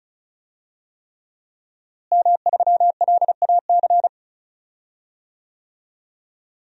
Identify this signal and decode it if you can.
Morse code, heard as a keyed tone, about 36 wpm, 700 Hz. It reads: M3LAC
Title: DXCC-100 Call Signs - 5 Characters - Repeated 3x - 35wpm